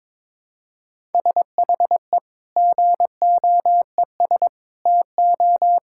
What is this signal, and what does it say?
Morse code, heard as a keyed tone, about 22 wpm, 700 Hz. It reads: SHE GOES TO
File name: practice / 22wpm / Sentences with 2-Word Phrases - 22wpm